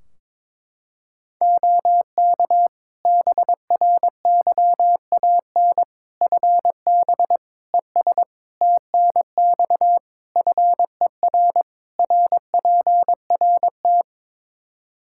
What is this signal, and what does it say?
Morse code, heard as a keyed tone, about 21 wpm, 700 Hz. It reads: OK BRYAN FB ES TNX FER RPRT
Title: Practice QSO - Exchange 2 - Info - 22wpm